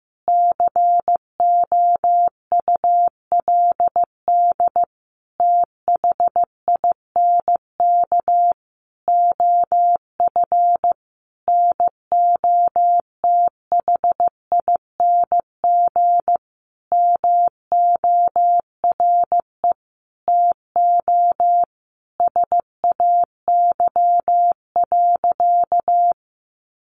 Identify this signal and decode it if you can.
Morse code, heard as a keyed tone, about 15 wpm, 700 Hz. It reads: COULD THINK OF NOTHING MORE TO SAY.
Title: Sentences from Top 500 Words 15wpm